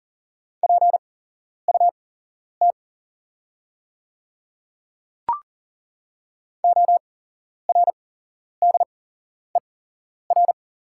Morse code, heard as a keyed tone, about 40 wpm, spaced as for 11 wpm, 690 Hz. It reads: PUT ORDER
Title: 8x Character Spacing - Top 500 Words - 40wpm